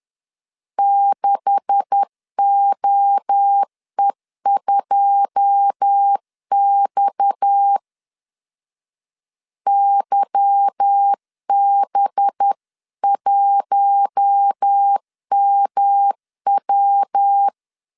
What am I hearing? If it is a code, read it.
Morse code, heard as a keyed tone, about 10 wpm, 800 Hz. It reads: 6OE2X YB1MW